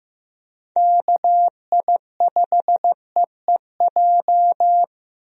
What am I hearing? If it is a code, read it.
Morse code, heard as a keyed tone, about 16 wpm, 700 Hz. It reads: KI5EEJ
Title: USA Call Signs - 6 Characters - Repeated 3x - 15wpm